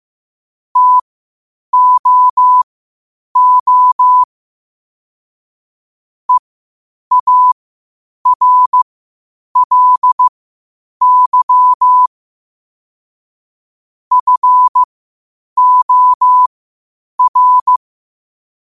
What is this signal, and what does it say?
Morse code, heard as a keyed tone, about 15 wpm, spaced as for 8 wpm, 1000 Hz. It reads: TOO EARLY FOR
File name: 8wpm